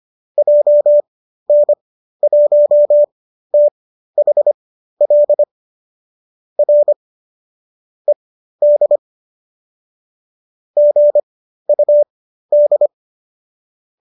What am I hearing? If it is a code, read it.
Morse code, heard as a keyed tone, about 25 wpm, spaced as for 13 wpm, 570 Hz. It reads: JN1THL R ED GUD